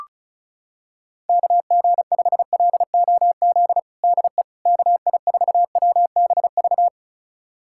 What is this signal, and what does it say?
Morse code, heard as a keyed tone, about 34 wpm, 700 Hz. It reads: KG5LOZ DE KI4WBV
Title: USA Callsign DE Callsign - 6 Characters - 35wpm